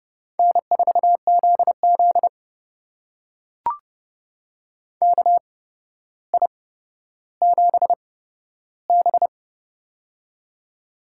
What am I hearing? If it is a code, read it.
Morse code, heard as a keyed tone, about 30 wpm, spaced as for 13 wpm, 700 Hz. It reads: N4ZZ KI7B
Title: USA Call Signs - 4 Characters - 8x Character Spacing - 30wpm